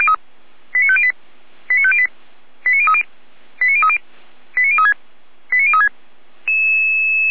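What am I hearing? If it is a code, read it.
Five-tone selective call (ZVEI), heard as quick runs of five tones, tones of about 70 ms. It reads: 89508 89508 89030 89030 89036 89036
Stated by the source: ZVEI five-tone call received by SDR.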